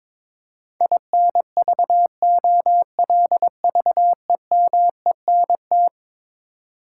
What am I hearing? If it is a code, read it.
Morse code, heard as a keyed tone, about 21 wpm, 700 Hz. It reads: INVOLVEMENT
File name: practice / 22wpm / Eleven Letter Words - 22wpm